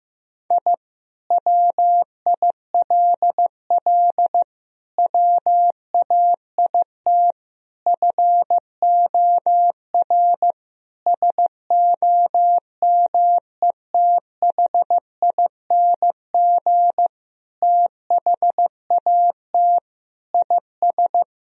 Morse code, heard as a keyed tone, about 16 wpm, 700 Hz. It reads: I WILL WAIT FOR SOMETHING THAT IS
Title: Sentences from Top 1000 Words - Review Part 1 - 15wpm